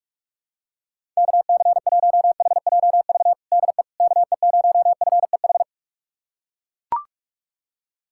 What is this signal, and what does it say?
Morse code, heard as a keyed tone, about 41 wpm, 700 Hz. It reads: KC1HJV DE KE0FEH
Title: USA Callsign DE Callsign 45wpm